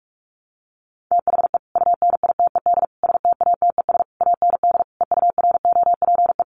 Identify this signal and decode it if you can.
Morse code, heard as a keyed tone, about 44 wpm, 700 Hz. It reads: THE UNITED STATES AND EUROPE
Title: Sets of 5 Words Encore 45wpm